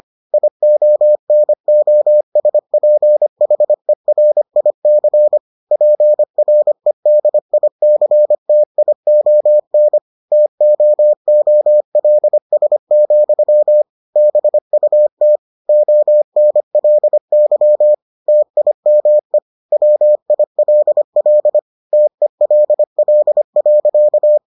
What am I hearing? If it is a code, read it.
Morse code, heard as a keyed tone, about 25 wpm, 600 Hz. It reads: IONOSPHERIC PREDICTION TOOLS, BUT ONLY TIME WILL TELL.